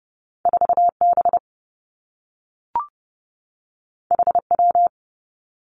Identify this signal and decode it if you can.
Morse code, heard as a keyed tone, about 30 wpm, 700 Hz. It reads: 4 B H W
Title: Two Challenging Letter-Number Combinations 30wpm